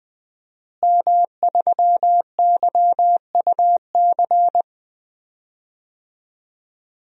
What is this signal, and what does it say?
Morse code, heard as a keyed tone, about 20 wpm, 700 Hz. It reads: M3YUC
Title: DXCC-100 Call Signs - 5 Characters - Repeated 3x - 20wpm